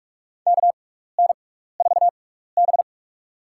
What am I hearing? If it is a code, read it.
Morse code, heard as a keyed tone, about 46 wpm, spaced as for 15 wpm, 700 Hz. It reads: KN4B
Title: USA Call Signs - 4 Characters - 6x Character Spacing - 45wpm